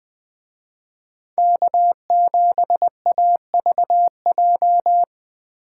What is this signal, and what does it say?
Morse code, heard as a keyed tone, about 21 wpm, 700 Hz. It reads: K7AVJ